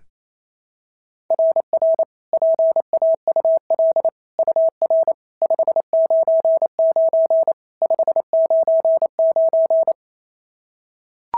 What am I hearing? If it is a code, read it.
Morse code, heard as a keyed tone, about 27 wpm, 660 Hz. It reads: RR PAUL UR 599 599
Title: SOTA QSO Practice 28wpm